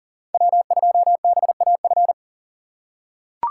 Morse code, heard as a keyed tone, about 38 wpm, 700 Hz. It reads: W2BAF
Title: USA Call Signs - 3x Character Spacing - 40wpm